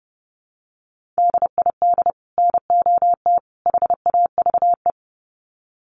Morse code, heard as a keyed tone, about 30 wpm, 700 Hz. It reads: DID NOT HAVE
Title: Sets of 3 Words 30wpm